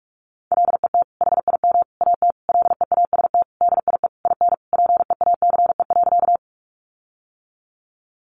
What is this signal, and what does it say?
Morse code, heard as a keyed tone, about 44 wpm, 700 Hz. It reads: LET HIM AT LEAST DIE IN PEACE.
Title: Sentences from Top 1500 Words 45wpm